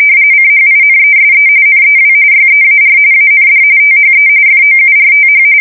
RTTY, about 45.45 baud. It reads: CHINESE NEW YEAR AND THE FOURTH
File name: fsk170 / USS-New-Jersey